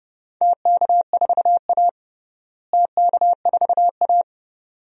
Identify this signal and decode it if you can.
Morse code, heard as a keyed tone, about 30 wpm, 700 Hz. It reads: TK4A TK4A
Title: DXCC-100 Call Signs - 4 Characters - Repeated 2x - 30wpm